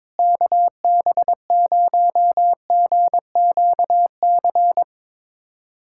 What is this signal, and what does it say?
Morse code, heard as a keyed tone, about 22 wpm, 700 Hz. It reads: KB0GQC
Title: USA Callsign DE Callsign - 6 Characters - 22wpm